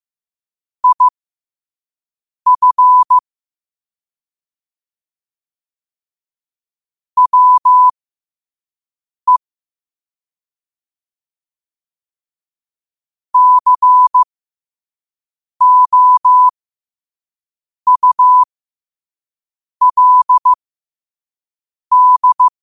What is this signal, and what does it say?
Morse code, heard as a keyed tone, about 15 wpm, spaced as for 5 wpm, 1000 Hz. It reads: IF WE COULD
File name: qotdfiles / 5wpm